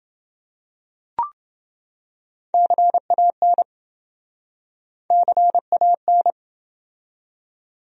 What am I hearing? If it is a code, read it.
Morse code, heard as a keyed tone, about 27 wpm, 700 Hz. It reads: CAN CAN